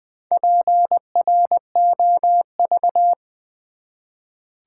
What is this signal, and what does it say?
Morse code, heard as a keyed tone, about 20 wpm, 700 Hz. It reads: PROV
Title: Build A Word - 1 Letter At A Time - 5-Letter Words - 20wpm